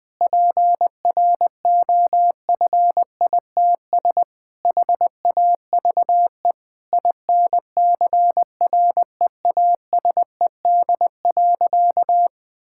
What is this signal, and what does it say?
Morse code, heard as a keyed tone, about 20 wpm, 700 Hz. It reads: PROFITS HAVE INCREASED.
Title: Sentences with 6-Word Phrases - Spoken Beginning - 20wpm